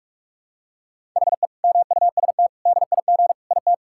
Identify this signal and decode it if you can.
Morse code, heard as a keyed tone, about 46 wpm, 700 Hz. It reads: HE MUST DIG IT
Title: Sentences from Top 1600 Words 45wpm